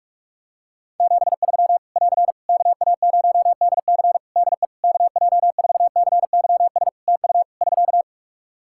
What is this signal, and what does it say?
Morse code, heard as a keyed tone, about 46 wpm, 700 Hz. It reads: 73 <AR> KA0DC DE KJ4CYS TU <SK>
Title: K1USN SST QSO Practice 45wpm